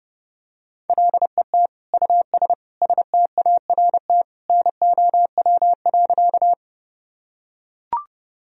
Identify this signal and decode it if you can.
Morse code, heard as a keyed tone, about 30 wpm, 700 Hz. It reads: LET US START NOW.
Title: Sentences from Top 1000 Words - Review Part 1 - 30wpm